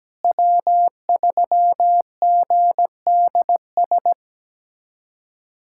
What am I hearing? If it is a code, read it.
Morse code, heard as a keyed tone, about 17 wpm, 700 Hz. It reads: W3GDS
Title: USA Callsign DE Callsign 17wpm